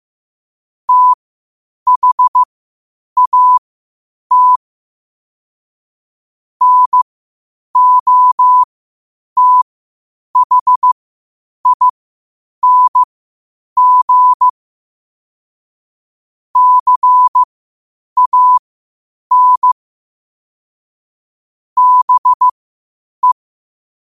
Morse code, heard as a keyed tone, about 15 wpm, spaced as for 8 wpm, 1000 Hz. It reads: THAT NOTHING CAN BE
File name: qotdfiles / 8wpm